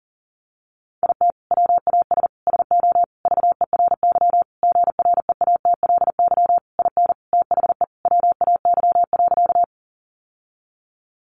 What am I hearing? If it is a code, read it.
Morse code, heard as a keyed tone, about 40 wpm, 700 Hz. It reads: IT WAS SO VERY GREATLY IN THE WAY.
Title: Sentences from Top 1400 Words 40wpm